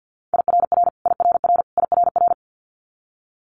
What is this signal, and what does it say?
Morse code, heard as a keyed tone, about 50 wpm, 700 Hz. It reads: IRR IRR IRR
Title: Multi-letter Phonemes Repeated 3x 50wpm